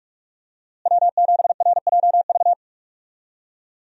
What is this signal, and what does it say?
Morse code, heard as a keyed tone, about 46 wpm, 700 Hz. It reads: W7RJV
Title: USA Call Signs - 5 Characters - Repeated 3x - 45wpm